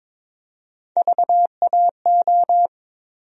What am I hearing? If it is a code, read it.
Morse code, heard as a keyed tone, about 22 wpm, 700 Hz. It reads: VAO
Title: USA Call Sign Postfixes - 3 Letters 22wpm